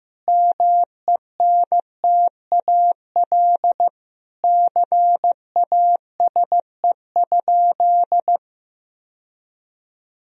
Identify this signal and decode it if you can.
Morse code, heard as a keyed tone, about 15 wpm, 700 Hz. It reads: MENTAL CASE?
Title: Sentences from Top 1000 Words - Review Part 4 - 15wpm